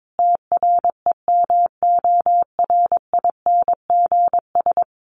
Morse code, heard as a keyed tone, about 22 wpm, 700 Hz. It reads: TREMORINGS
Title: A Sound Of Thunder 22wpm